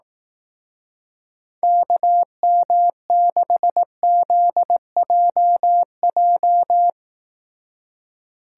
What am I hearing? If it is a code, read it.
Morse code, heard as a keyed tone, about 18 wpm, 700 Hz. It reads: KM6ZJJ
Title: Speed Racing - USA 2x3 Call Signs 15wpm